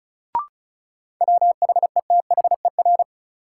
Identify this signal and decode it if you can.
Morse code, heard as a keyed tone, about 34 wpm, 700 Hz. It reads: WHETHER
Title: Sets of 3 Words - New Word Drill - 35wpm